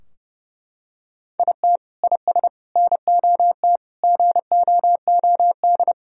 Morse code, heard as a keyed tone, about 30 wpm, 700 Hz. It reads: IT IS NOT GOOD
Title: Sentences from Top 1000 Words - Review Part 2 - 30wpm